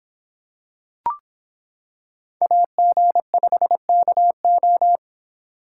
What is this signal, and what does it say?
Morse code, heard as a keyed tone, about 26 wpm, 700 Hz. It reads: AG5KO